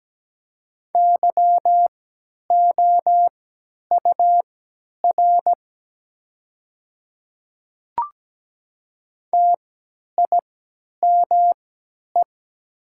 Morse code, heard as a keyed tone, about 17 wpm, spaced as for 10 wpm, 700 Hz. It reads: YOUR TIME
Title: 3x Character Spacing - Top 100 Words - 17wpm